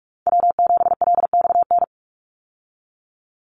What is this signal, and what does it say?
Morse code, heard as a keyed tone, about 44 wpm, 700 Hz. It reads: W7LXN
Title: Call Signs Repeated 3x - 45wpm